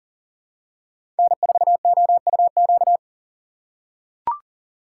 Morse code, heard as a keyed tone, about 40 wpm, 700 Hz. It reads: N4OUQ
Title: USA Call Signs - 4x Character Spacing - 40wpm